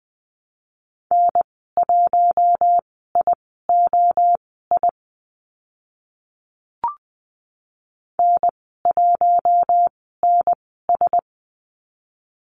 Morse code, heard as a keyed tone, about 20 wpm, spaced as for 15 wpm, 700 Hz. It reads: N1IOI N1NS
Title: 042 Lesson 8g - Number 1 - Call Signs 20wpm